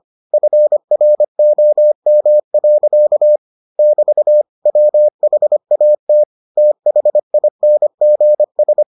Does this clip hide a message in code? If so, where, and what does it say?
Yes: Morse code from 0.3 to 8.8 s, heard as a keyed tone, about 25 wpm, 600 Hz. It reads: FROM. <BT> WHAT THINGS